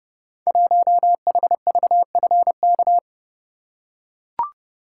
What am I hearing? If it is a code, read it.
Morse code, heard as a keyed tone, about 30 wpm, 700 Hz. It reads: 1HVFK
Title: Code Groups - 5 Letter:Numbers - 30wpm